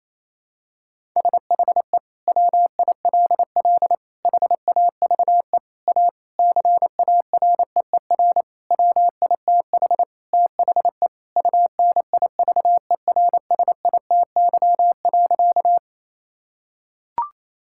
Morse code, heard as a keyed tone, about 27 wpm, 700 Hz. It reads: SHE WILL HAVE A CAREER WITH THE UNIVERSITY.
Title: Sentences from Top 1000 Words - Review Part 6 - 28wpm